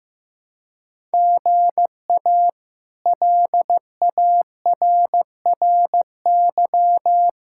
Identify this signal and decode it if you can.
Morse code, heard as a keyed tone, about 15 wpm, 700 Hz. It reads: GA LARRY
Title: K1USN SST QSO Practice - 15wpm